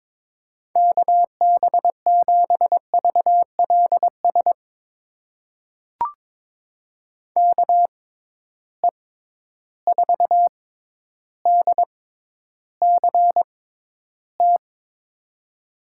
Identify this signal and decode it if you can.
Morse code, heard as a keyed tone, about 22 wpm, spaced as for 11 wpm, 700 Hz. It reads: KB7VLS KE4DCT
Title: USA Call Signs - 6 Characters - 6x Character Spacing - 22wpm